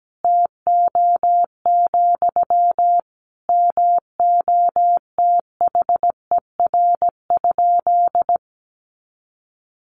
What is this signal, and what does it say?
Morse code, heard as a keyed tone, about 17 wpm, 700 Hz. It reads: TO, MOTHER?
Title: Sentences from Top 1000 Words 17wpm